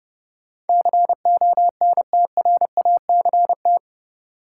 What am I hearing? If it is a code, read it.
Morse code, heard as a keyed tone, about 32 wpm, 700 Hz. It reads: CONTRACT
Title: Sentences from Top 1400 Words - New Word Drill - 30wpm